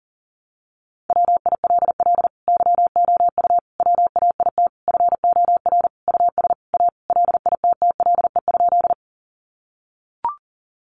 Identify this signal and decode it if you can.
Morse code, heard as a keyed tone, about 40 wpm, 700 Hz. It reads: WILL YOU WAIT FOR US A LITTLE?
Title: Sentences from Top 900 Words 40wpm